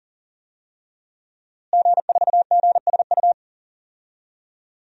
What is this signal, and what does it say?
Morse code, heard as a keyed tone, about 40 wpm, 700 Hz. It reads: G4GSU
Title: DXCC-100 Call Signs - 5 Characters - Repeated 3x - 40wpm